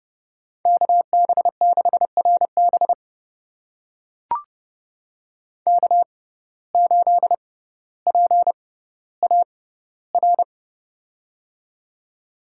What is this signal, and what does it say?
Morse code, heard as a keyed tone, about 31 wpm, spaced as for 16 wpm, 700 Hz. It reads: KB6RB K8PAR
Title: USA Call Signs - 5 Characters - 6x Character Spacing - 30wpm